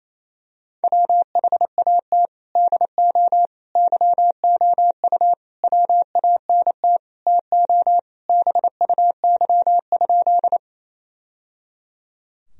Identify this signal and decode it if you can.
Morse code, heard as a keyed tone, about 28 wpm, 700 Hz. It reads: WHAT DO YOU WANT TO BUY?
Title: Sentences from Top 1000 Words - Review Part 4 - 28wpm